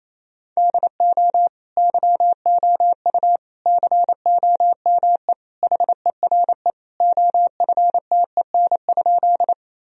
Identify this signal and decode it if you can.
Morse code, heard as a keyed tone, about 28 wpm, 700 Hz. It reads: DO YOU COME HERE OFTEN?
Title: Sentences from Top 1000 Words - Review Part 1 - 28wpm